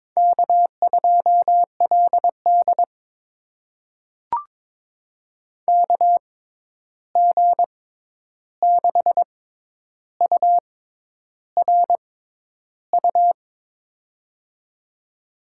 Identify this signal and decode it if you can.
Morse code, heard as a keyed tone, about 22 wpm, spaced as for 10 wpm, 700 Hz. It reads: K2LD KG6URU
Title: USA Call Signs - 6x Character Spacing - 22wpm